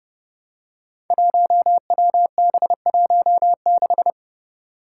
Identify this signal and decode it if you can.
Morse code, heard as a keyed tone, about 29 wpm, 700 Hz. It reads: 1WB16
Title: Five Challenging Letter-Number Combinations 30wpm